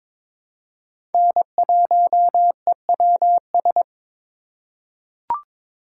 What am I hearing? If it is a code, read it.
Morse code, heard as a keyed tone, about 21 wpm, 700 Hz. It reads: N1EWS
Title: USA Call Signs - 5 Characters - 4x Character Spacing - 22wpm